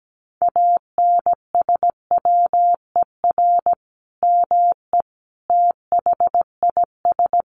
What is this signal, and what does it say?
Morse code, heard as a keyed tone, about 17 wpm, 700 Hz. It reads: ANSWER ME THIS
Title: Sentences from Top 400 Words 17wpm